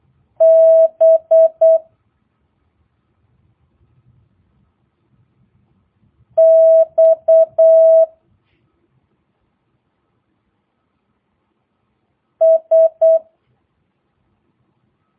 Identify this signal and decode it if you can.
Morse code, heard as a keyed tone, about 8 wpm, 650 Hz. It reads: B X S
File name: jh8cbh / 2022cw